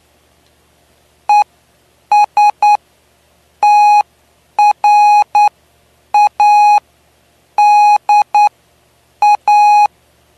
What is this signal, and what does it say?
Morse code, heard as a keyed tone, about 9 wpm, spaced as for 7 wpm, 830 Hz. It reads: ESTRADA
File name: cw / py2lcd / 09-PY2LCD-Li��o 9